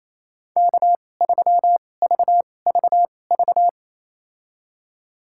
Morse code, heard as a keyed tone, about 28 wpm, spaced as for 20 wpm, 700 Hz. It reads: K3VVV
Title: USA Call Signs - 2x Character Spacing - 28wpm